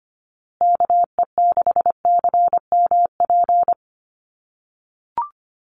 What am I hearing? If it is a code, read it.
Morse code, heard as a keyed tone, about 25 wpm, 700 Hz. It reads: KE6CMP
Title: USA Call Signs Repeated 3x 25wpm